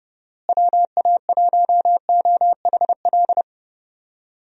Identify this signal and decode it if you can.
Morse code, heard as a keyed tone, about 31 wpm, 700 Hz. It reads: WA1OHL
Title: USA Call Signs - 6 Characters - Repeated 2x - 30wpm